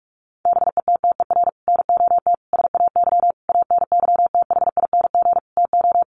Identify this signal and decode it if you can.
Morse code, heard as a keyed tone, about 46 wpm, 700 Hz. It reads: BETTER NOT SAY ANYTHING TO
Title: Sentences from Top 1000 Words 45wpm